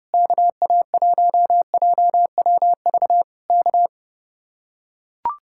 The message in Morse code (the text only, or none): KA1JWV K